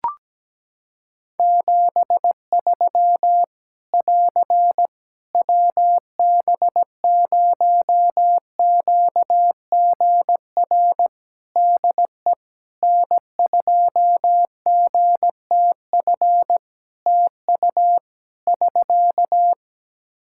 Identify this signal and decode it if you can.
Morse code, heard as a keyed tone, about 17 wpm, 700 Hz. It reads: 73 <AR> WB0QGR DE N2GTF TU <SK>
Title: Practice QSO - Exchange 4 - Ending - 17wpm